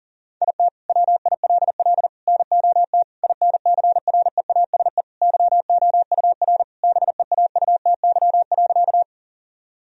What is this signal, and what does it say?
Morse code, heard as a keyed tone, about 40 wpm, 700 Hz. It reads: IT WILL NOT INCREASE YOUR BEAUTY.
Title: Sentences from Top 1000 Words - Review Part 4 - 40wpm